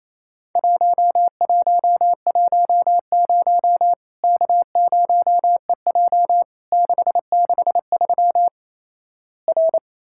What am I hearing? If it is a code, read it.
Morse code, heard as a keyed tone, about 28 wpm, 700 Hz. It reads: 1110 K0EJ 663 R
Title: Worked All Europe DX Contest - QTC Exchange - 28wpm